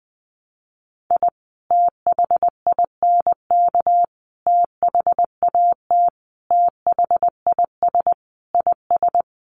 Morse code, heard as a keyed tone, about 20 wpm, 700 Hz. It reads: I THINK THAT THIS IS
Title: Sets of 5 Words Encore2 20wpm